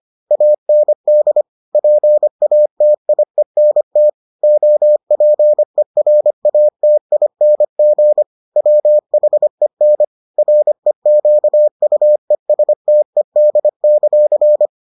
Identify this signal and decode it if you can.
Morse code, heard as a keyed tone, about 25 wpm, 600 Hz. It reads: AND PATIENT OPERATING WHEN REQUESTED;